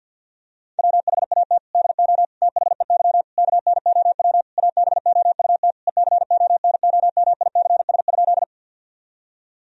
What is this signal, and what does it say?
Morse code, heard as a keyed tone, about 50 wpm, 700 Hz. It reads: WHAT DO THEY KNOW ABOUT ECONOMICS?
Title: Sentences from Top 1000 Words - Review Part 3 - 50wpm